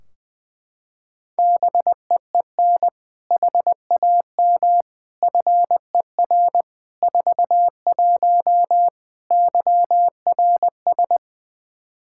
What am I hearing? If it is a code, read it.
Morse code, heard as a keyed tone, about 20 wpm, 700 Hz. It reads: BEEN HAM FER 41 YRS
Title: K1USN SST QSO Practice 20wpm